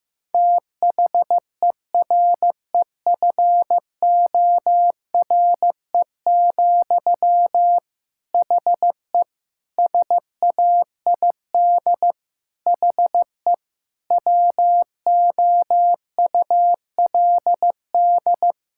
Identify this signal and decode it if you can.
Morse code, heard as a keyed tone, about 16 wpm, 700 Hz. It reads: THEREFORE, HE SAID HE WOULD
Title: Sentences from Top 1000 Words - Review Part 5 - 15wpm